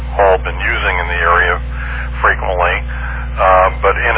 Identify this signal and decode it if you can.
Emergency Alert System (SAME) tones, heard as harsh buzzing bursts, with speech. It is heard from 0.6 to 3.2 s.